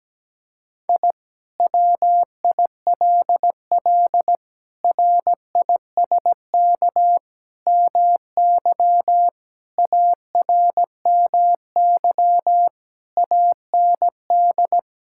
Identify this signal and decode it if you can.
Morse code, heard as a keyed tone, about 17 wpm, 700 Hz. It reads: I WILL RISK MY ARMY AND
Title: Sentences from Top 1000 Words - Review Part 5 - 17wpm